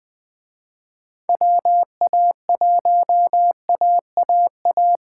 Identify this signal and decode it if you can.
Morse code, heard as a keyed tone, about 20 wpm, 700 Hz. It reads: WA1AAA
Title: Speed Racing - USA 2x3 Call Signs 17wpm